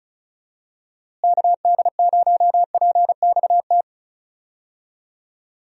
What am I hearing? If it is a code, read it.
Morse code, heard as a keyed tone, about 34 wpm, 700 Hz. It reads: KD0PXT
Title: USA Call Signs - 6 Characters - Repeated 2x - 35wpm